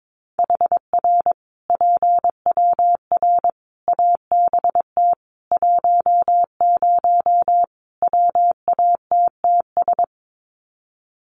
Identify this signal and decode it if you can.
Morse code, heard as a keyed tone, about 22 wpm, 700 Hz. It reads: HR PWR ABT 10 WATTS
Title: Common QSO Phrases 22wpm